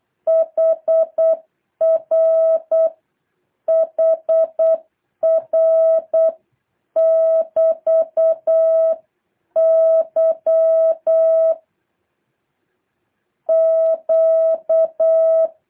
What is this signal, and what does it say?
Morse code, heard as a keyed tone, about 8 wpm, 660 Hz. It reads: HR HR <BT> Y Q
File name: jh8cbh / 2023cw